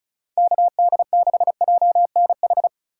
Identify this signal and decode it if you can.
Morse code, heard as a keyed tone, about 36 wpm, 700 Hz. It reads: KD6JNH
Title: USA Call Signs - 3x Character Spacing - 35wpm